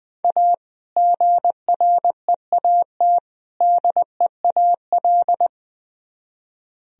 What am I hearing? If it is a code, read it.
Morse code, heard as a keyed tone, about 22 wpm, 700 Hz. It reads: A GREAT DEAL
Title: Speed-Racing - Sets of 3 Words 20wpm